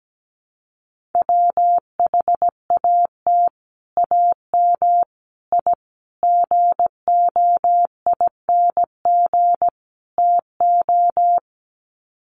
Morse code, heard as a keyed tone, about 17 wpm, 700 Hz. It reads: WHAT AM I GOING TO
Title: Sets of 5 Words Encore 17wpm